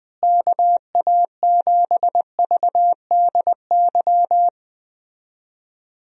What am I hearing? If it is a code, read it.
Morse code, heard as a keyed tone, about 20 wpm, 700 Hz. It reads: KA7VDY
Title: USA Call Signs - 6 Characters - Repeated 2x - 20wpm